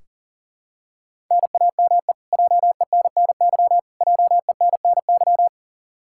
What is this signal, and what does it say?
Morse code, heard as a keyed tone, about 40 wpm, 700 Hz. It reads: NAME JENNY JENNY
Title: Practice QSO - Exchange 1 - Beginning - 40wpm